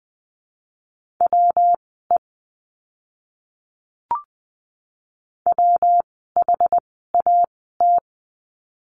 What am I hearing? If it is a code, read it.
Morse code, heard as a keyed tone, about 20 wpm, spaced as for 15 wpm, 700 Hz. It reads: WE WHAT
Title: 115 Lesson 19f - Letter w - Words 20wpm